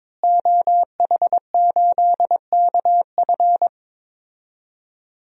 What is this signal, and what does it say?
Morse code, heard as a keyed tone, about 22 wpm, 700 Hz. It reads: OH8KF
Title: DXCC-100 Call Signs - 5 Characters - Repeated 3x - 22wpm